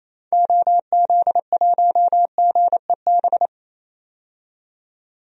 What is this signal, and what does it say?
Morse code, heard as a keyed tone, about 27 wpm, 700 Hz. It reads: OZ1GEB
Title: DXCC-100 Call Signs - 6 Characters - 28wpm